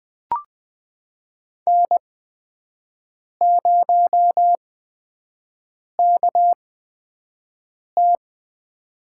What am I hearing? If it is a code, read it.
Morse code, heard as a keyed tone, about 20 wpm, spaced as for 5 wpm, 700 Hz. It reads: N0KT
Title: USA Call Signs - 8x Character Spacing - 20wpm